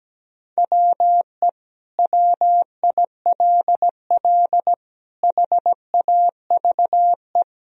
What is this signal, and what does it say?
Morse code, heard as a keyed tone, about 17 wpm, 700 Hz. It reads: WE WILL HAVE
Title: Sentences from Top 1000 Words - Review Part 1 - 17wpm